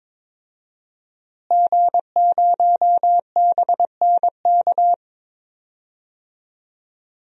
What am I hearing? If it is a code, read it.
Morse code, heard as a keyed tone, about 22 wpm, 700 Hz. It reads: G0BNK
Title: DXCC-100 Call Signs - 5 Characters - Repeated 2x - 22wpm